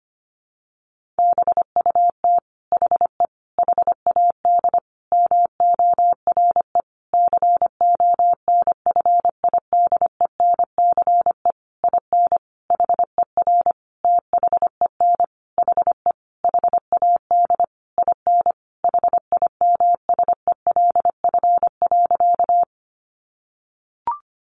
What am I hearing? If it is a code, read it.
Morse code, heard as a keyed tone, about 25 wpm, 700 Hz. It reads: BUT HE HAD MORE CONFIDENCE IN HER THEN HE HAD IN HIMSELF.
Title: Sentences from Top 1500 Words 25wpm